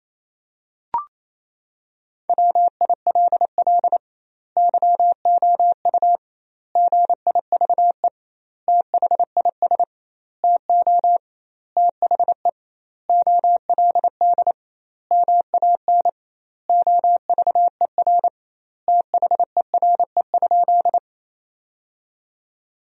Morse code, heard as a keyed tone, about 27 wpm, 700 Hz. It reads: WILL YOU GIVE THIS TO THE OLD MAN OVER THERE?
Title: 2x Word-Spacing - Sentences from Top 200 Words - 28wpm